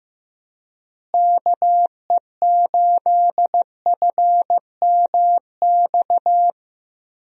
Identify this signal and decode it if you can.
Morse code, heard as a keyed tone, about 15 wpm, 700 Hz. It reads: KE8FMX
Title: USA Call Signs - 6 Characters - Repeated 3x - 15wpm